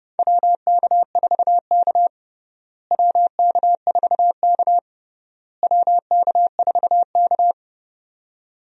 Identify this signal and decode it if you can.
Morse code, heard as a keyed tone, about 30 wpm, 700 Hz. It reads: WK4K WK4K WK4K
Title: USA Call Signs - 4 Characters - Repeated 3x - 30wpm